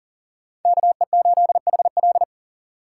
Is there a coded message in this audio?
Yes: Morse code from 0.7 to 2.2 s, heard as a keyed tone, about 50 wpm, 700 Hz. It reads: KE8HL